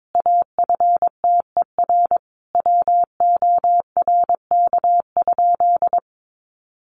Morse code, heard as a keyed tone, about 22 wpm, 700 Hz. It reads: AFTER WORK?
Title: Sentences from Top 1400 Words 22wpm